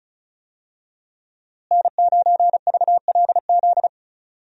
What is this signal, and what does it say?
Morse code, heard as a keyed tone, about 36 wpm, 700 Hz. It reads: N9VLZ
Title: USA Call Signs - 5 Characters - Repeated 2x - 35wpm